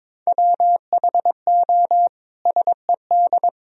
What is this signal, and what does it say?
Morse code, heard as a keyed tone, about 22 wpm, 700 Hz. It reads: WHO SED
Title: Sentences from Top 300 Words and Abbreviations 22wpm